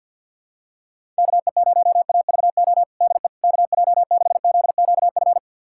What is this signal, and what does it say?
Morse code, heard as a keyed tone, about 51 wpm, 700 Hz. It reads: KE0AVO DE KJ6ZQF
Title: USA Callsign DE Callsign 50wpm